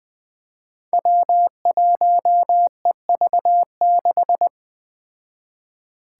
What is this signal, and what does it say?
Morse code, heard as a keyed tone, about 20 wpm, 700 Hz. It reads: W1EV6
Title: Five Challenging Letter-Number Combinations 20wpm